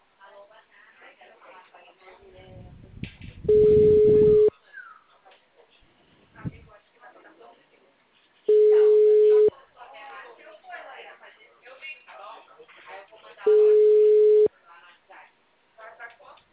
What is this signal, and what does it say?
Ringback tone (Europe, 425 Hz), with speech.
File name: out- -2003